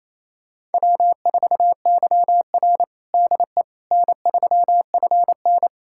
Morse code, heard as a keyed tone, about 28 wpm, 700 Hz. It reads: W4YR DE N3FN
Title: USA Callsign DE Callsign - 4 Characters - 28wpm